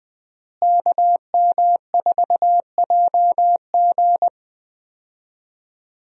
Morse code, heard as a keyed tone, about 20 wpm, 700 Hz. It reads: KM4JG
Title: USA Call Signs - 5 Characters - Repeated 2x - 20wpm